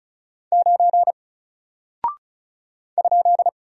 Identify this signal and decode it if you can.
Morse code, heard as a keyed tone, about 36 wpm, 700 Hz. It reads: 9 ?
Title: Single Character (Letter:Number:Punctuation:Prosign) 35wpm